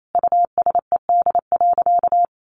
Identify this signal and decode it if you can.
Morse code, heard as a keyed tone, about 28 wpm, 700 Hz. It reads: USED.
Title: Sentences from Top 1500 Words 28wpm